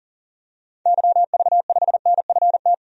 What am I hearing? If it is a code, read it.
Morse code, heard as a keyed tone, about 37 wpm, 700 Hz. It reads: YV5NFT
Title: DXCC-100 Call Signs - 6 Characters - 40wpm